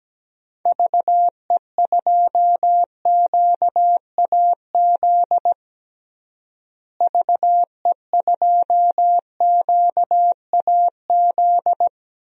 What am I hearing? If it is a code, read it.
Morse code, heard as a keyed tone, about 17 wpm, 700 Hz. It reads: VE2QAZ VE2QAZ
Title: DXCC-100 Call Signs - 6 Characters - Repeated 2x - 17wpm